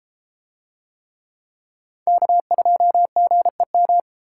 Morse code, heard as a keyed tone, about 31 wpm, 700 Hz. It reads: K2GEM